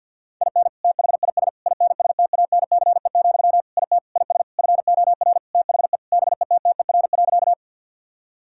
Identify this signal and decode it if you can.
Morse code, heard as a keyed tone, about 50 wpm, 700 Hz. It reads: IN THIS INSTANCE, IT IS FOR THE BETTER.
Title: Sentences from Top 1000 Words - Review Part 5 - 50wpm